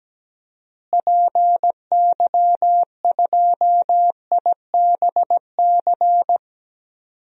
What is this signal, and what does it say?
Morse code, heard as a keyed tone, about 17 wpm, 700 Hz. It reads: PY2IBC
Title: DXCC-100 Call Signs - 6 Characters - Repeated 3x - 17wpm